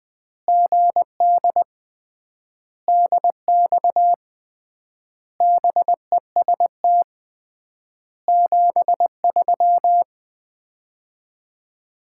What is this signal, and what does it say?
Morse code, heard as a keyed tone, about 20 wpm, 700 Hz. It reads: GD DX BEST 73
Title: Common QSO Phrases - 3x Word-Spacing - 20wpm